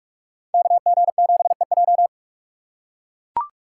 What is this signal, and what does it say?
Morse code, heard as a keyed tone, about 42 wpm, 700 Hz. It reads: KG7EJ